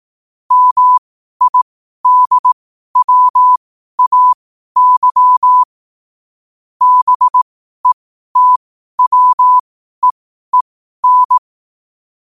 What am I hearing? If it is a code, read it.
Morse code, heard as a keyed tone, about 18 wpm, spaced as for 12 wpm, 1000 Hz. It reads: MIDWAY BETWEEN
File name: qotdfiles / 12wpm